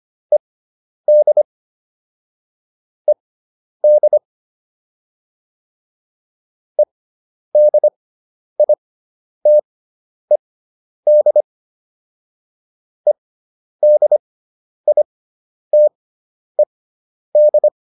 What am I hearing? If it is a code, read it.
Morse code, heard as a keyed tone, about 25 wpm, spaced as for 10 wpm, 600 Hz. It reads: ED ED EDITED EDITED